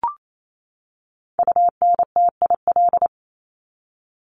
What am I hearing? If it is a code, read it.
Morse code, heard as a keyed tone, about 28 wpm, 700 Hz. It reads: UNTIL
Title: Top 500 Words 28wpm